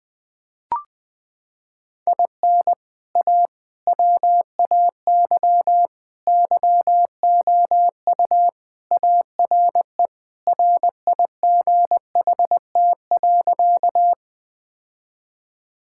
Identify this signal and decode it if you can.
Morse code, heard as a keyed tone, about 21 wpm, 700 Hz. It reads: IN A WAY YOU ARE RIGHT.
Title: Sentences with 3-Word Phrases - 20wpm